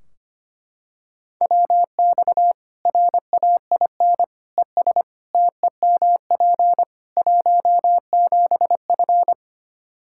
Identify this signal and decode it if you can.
Morse code, heard as a keyed tone, about 25 wpm, 700 Hz. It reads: WX RAIN ES TEMP 17F
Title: Practice QSO - Exchange 2 - Info - 25wpm